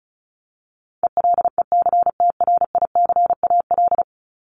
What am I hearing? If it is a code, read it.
Morse code, heard as a keyed tone, about 34 wpm, 700 Hz. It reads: ELECTRICAL
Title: Top 751-1000 Words 35wpm